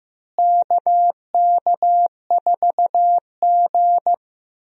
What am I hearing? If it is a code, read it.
Morse code, heard as a keyed tone, about 15 wpm, 700 Hz. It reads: KK4G
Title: USA Call Signs - 4 Characters - Repeated 3x - 15wpm